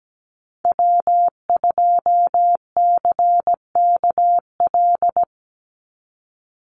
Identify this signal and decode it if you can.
Morse code, heard as a keyed tone, about 17 wpm, 700 Hz. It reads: W2CKL
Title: NA SOTA Activator Callsigns 17wpm